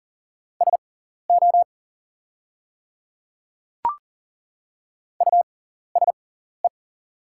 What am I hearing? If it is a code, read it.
Morse code, heard as a keyed tone, about 40 wpm, spaced as for 14 wpm, 700 Hz. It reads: SO USE
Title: 6x Character Spacing - Top 200 Words - 40wpm